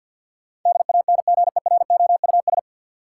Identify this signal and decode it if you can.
Morse code, heard as a keyed tone, about 53 wpm, 700 Hz. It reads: DANGEROUS
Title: Sentences from Top 1200 Words - New Word Drill - 50wpm